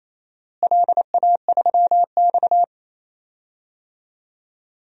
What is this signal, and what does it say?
Morse code, heard as a keyed tone, about 28 wpm, 700 Hz. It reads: LA3X
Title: DXCC-100 Call Signs - 4 Characters - Repeated 2x - 28wpm